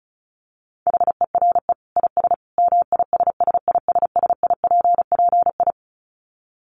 Morse code, heard as a keyed tone, about 34 wpm, 700 Hz. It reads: HERE IS MISSISSIPPI
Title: Dit Challenge 35wpm